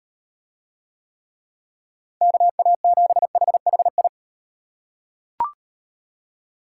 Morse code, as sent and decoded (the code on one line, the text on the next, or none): -.- .- --... .... .... ..
KA7HHI